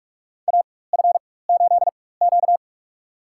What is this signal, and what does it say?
Morse code, heard as a keyed tone, about 45 wpm, spaced as for 21 wpm, 700 Hz. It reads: AF8Q
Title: USA Call Signs - 4 Characters - 4x Character Spacing - 45wpm